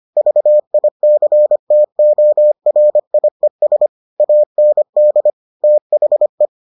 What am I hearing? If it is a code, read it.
Morse code, heard as a keyed tone, about 25 wpm, 600 Hz. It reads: VICTORIES AND THE